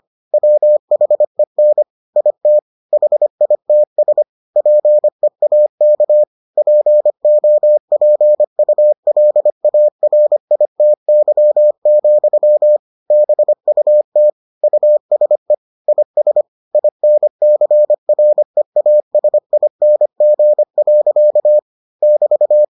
Morse code, heard as a keyed tone, about 25 wpm, 600 Hz. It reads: WHEN IT HITS PEAK POPULARITY, BUT USE IS INCREASING. <BT>